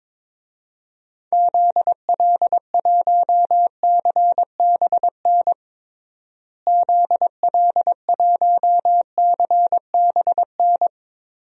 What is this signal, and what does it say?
Morse code, heard as a keyed tone, about 22 wpm, 700 Hz. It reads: ZL1CBN ZL1CBN
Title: DXCC-100 Call Signs - 6 Characters - Repeated 3x - 22wpm